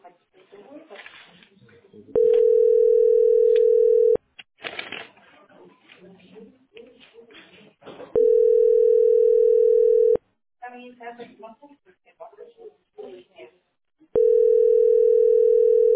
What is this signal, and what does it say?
Ringback tone (North America), 440 + 480 Hz, with speech.
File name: ce6f425e-19e6-43f5-a331